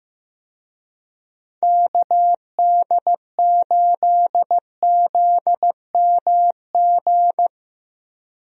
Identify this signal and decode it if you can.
Morse code, heard as a keyed tone, about 15 wpm, 700 Hz. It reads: KD8ZMG
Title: USA Call Signs - 6 Characters - Repeated 2x - 15wpm